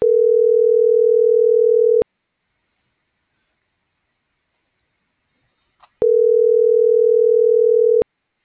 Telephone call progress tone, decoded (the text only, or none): ringback tone